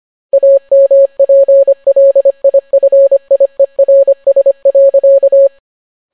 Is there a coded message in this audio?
Yes: Morse code from 0.3 to 5.5 s, heard as a keyed tone, about 25 wpm, 550 Hz. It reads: AMPLIFIERS.